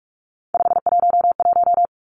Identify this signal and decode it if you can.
Morse code, heard as a keyed tone, about 46 wpm, 700 Hz. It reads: 511
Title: Three Challenging Letter-Number Combinations 45wpm